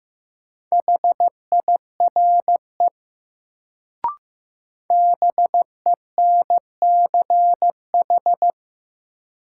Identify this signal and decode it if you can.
Morse code, heard as a keyed tone, about 15 wpm, 700 Hz. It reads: HIRE BENCH
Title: Sentences from Top 1400 Words - New Word Drill - 15wpm